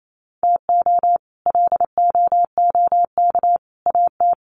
Morse code, heard as a keyed tone, about 28 wpm, 700 Hz. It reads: TO LOOK AT
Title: Sets of 3 Words 28wpm